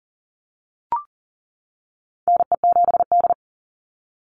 Morse code, heard as a keyed tone, about 40 wpm, 700 Hz. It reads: NE7D
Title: Call Signs - 40wpm